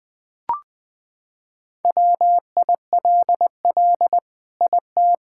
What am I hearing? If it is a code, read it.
Morse code, heard as a keyed tone, about 20 wpm, 700 Hz. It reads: WILL IT
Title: Sentences from Top 1000 Words - Review Part 5 - 20wpm